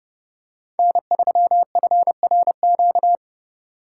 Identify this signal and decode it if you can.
Morse code, heard as a keyed tone, about 30 wpm, 700 Hz. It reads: N3FRQ
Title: USA Call Signs - 4x Character Spacing - 30wpm